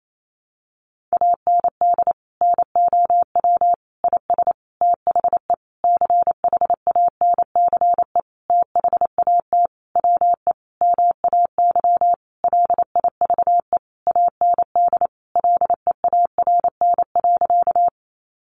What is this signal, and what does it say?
Morse code, heard as a keyed tone, about 28 wpm, 700 Hz. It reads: AND NOW IS THE CHANCE THAT WE MAY LIVE AND LEARN.
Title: Sentences from Top 800 Words 28wpm